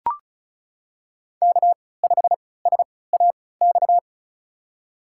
Morse code, heard as a keyed tone, about 34 wpm, spaced as for 20 wpm, 700 Hz. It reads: K5SAX
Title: USA Call Signs - 5 Characters - 3x Character Spacing - 35wpm